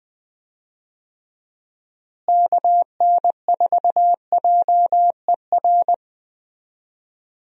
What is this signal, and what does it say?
Morse code, heard as a keyed tone, about 20 wpm, 700 Hz. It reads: KN4JER
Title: USA Call Signs - 6 Characters - Repeated 2x - 20wpm